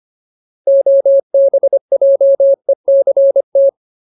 Morse code, heard as a keyed tone, about 25 wpm, 550 Hz. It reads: OBJECT
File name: Ultimate Word List / 25WPM Ultimate